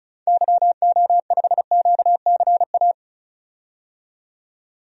Morse code, heard as a keyed tone, about 36 wpm, 700 Hz. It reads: YO5QCA
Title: DXCC-100 Call Signs - 6 Characters - 35wpm